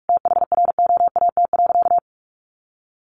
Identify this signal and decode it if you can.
Morse code, heard as a keyed tone, about 44 wpm, 700 Hz. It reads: THROAT.
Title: A Sound Of Thunder 45wpm